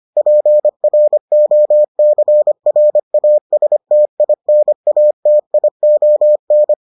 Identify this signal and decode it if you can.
Morse code, heard as a keyed tone, about 25 wpm, 600 Hz. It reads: PROCRASTINATION